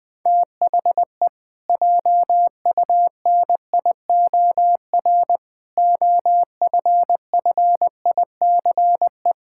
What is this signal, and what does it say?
Morse code, heard as a keyed tone, about 21 wpm, 700 Hz. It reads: THE JUNIOR OFFICE
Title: Pangram Challenge v2 - 20wpm